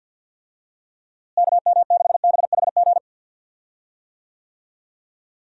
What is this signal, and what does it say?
Morse code, heard as a keyed tone, about 50 wpm, 700 Hz. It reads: KM6BHG
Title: USA Call Signs - 6 Characters - Repeated 3x - 50wpm